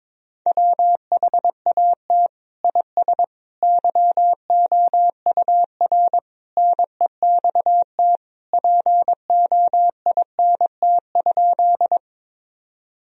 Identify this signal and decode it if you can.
Morse code, heard as a keyed tone, about 22 wpm, 700 Hz. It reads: WHAT IS YOUR NEXT POINT?
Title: Sentences from Top 1000 Words - Review Part 1 - 22wpm